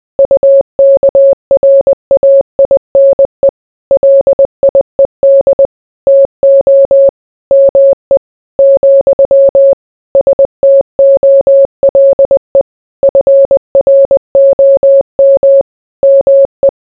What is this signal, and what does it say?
Morse code, heard as a keyed tone, about 20 wpm, 550 Hz. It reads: UKRAINE LIED TO ME , STOLE FROM ME